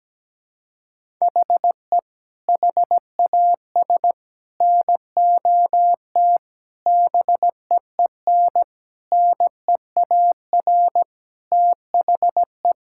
Morse code, heard as a keyed tone, about 17 wpm, 700 Hz. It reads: HE HAS NOT BEEN NEAR THE
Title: Sentences from Top 1000 Words - Review Part 6 - 17wpm